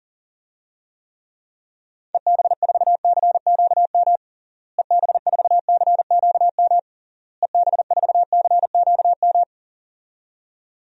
Morse code, heard as a keyed tone, about 40 wpm, 700 Hz. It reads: EB4CQM EB4CQM EB4CQM
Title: DXCC-100 Call Signs - 6 Characters - Repeated 3x - 40wpm